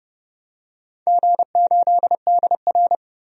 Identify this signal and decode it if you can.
Morse code, heard as a keyed tone, about 30 wpm, 700 Hz. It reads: G8DR
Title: DXCC-100 Call Signs - 4 Characters - 30wpm